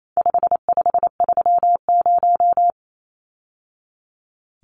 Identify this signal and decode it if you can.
Morse code, heard as a keyed tone, about 28 wpm, 700 Hz. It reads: 5530
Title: Numbers - 4-Digit 28wpm